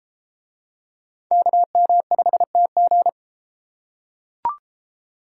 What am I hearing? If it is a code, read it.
Morse code, heard as a keyed tone, about 30 wpm, 700 Hz. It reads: KM5TG